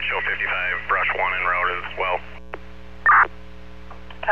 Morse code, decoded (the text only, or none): none